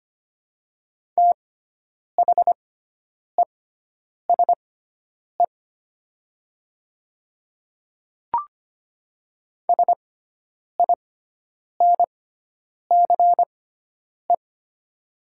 Morse code, heard as a keyed tone, about 25 wpm, spaced as for 9 wpm, 700 Hz. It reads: THESE SINCE
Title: 6x Character Spacing - Top 300 Words - 25wpm